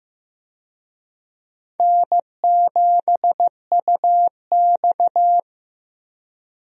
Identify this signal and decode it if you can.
Morse code, heard as a keyed tone, about 15 wpm, 700 Hz. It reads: N7UX
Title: USA Call Signs - 4 Characters - 15wpm